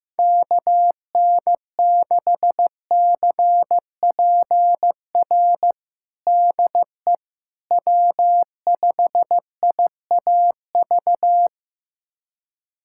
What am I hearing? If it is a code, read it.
Morse code, heard as a keyed tone, about 16 wpm, 700 Hz. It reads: KN6CPR DE W5IAV
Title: USA Callsign DE Callsign 15wpm